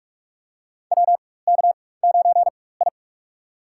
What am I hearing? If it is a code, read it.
Morse code, heard as a keyed tone, about 46 wpm, spaced as for 21 wpm, 700 Hz. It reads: WK9I
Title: USA Call Signs - 4 Characters - 4x Character Spacing - 45wpm